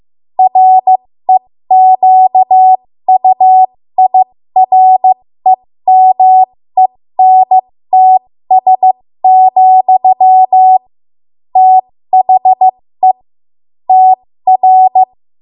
Morse code, heard as a keyed tone, about 15 wpm, spaced as for 13 wpm, 750 Hz. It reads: REQUIREMENTS, THE TR
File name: Morse / 13WPM